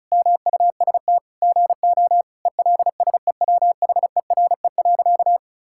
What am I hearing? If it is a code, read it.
Morse code, heard as a keyed tone, about 36 wpm, 700 Hz. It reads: MUST GO ELSEWHERE.
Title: Sentences from Top 1600 Words 35wpm